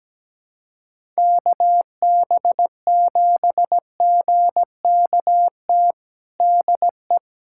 Morse code, heard as a keyed tone, about 17 wpm, 700 Hz. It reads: KB7GKT DE
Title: USA Callsign DE Callsign 17wpm